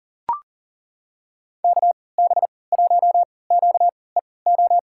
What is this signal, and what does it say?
Morse code, heard as a keyed tone, about 37 wpm, spaced as for 22 wpm, 700 Hz. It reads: KB1QEO
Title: USA Call Signs - 3x Character Spacing - 40wpm